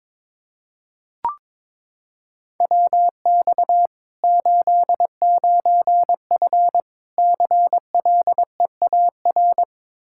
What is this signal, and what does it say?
Morse code, heard as a keyed tone, about 22 wpm, 700 Hz. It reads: WX 89F CLEAR
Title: Practice QSO - Exchange 2 - Info - 22wpm